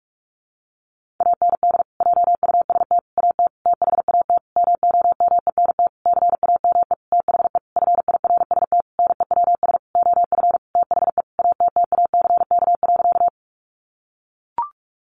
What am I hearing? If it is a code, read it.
Morse code, heard as a keyed tone, about 44 wpm, 700 Hz. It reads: AND JUST AT THAT MOMENT CAME THE FIRST NEWS OF THE ATTACK.
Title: Sentences from Top 1500 Words 45wpm